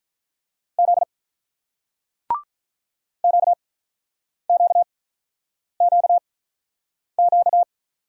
Morse code, heard as a keyed tone, about 48 wpm, spaced as for 9 wpm, 700 Hz. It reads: Z QQQQ